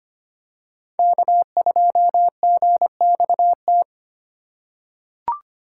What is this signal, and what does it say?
Morse code, heard as a keyed tone, about 25 wpm, 700 Hz. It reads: K2GXT
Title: USA Call Signs - 8x Character Spacing - 25wpm